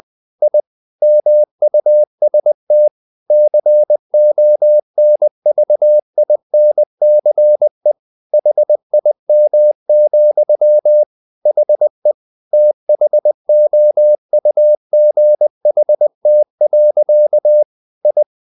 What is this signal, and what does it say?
Morse code, heard as a keyed tone, about 20 wpm, 600 Hz. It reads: I MUST CONVINCE HIM, HE THOUGHT. I
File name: the old man and the sea / the old man and the sea part 1 20wpm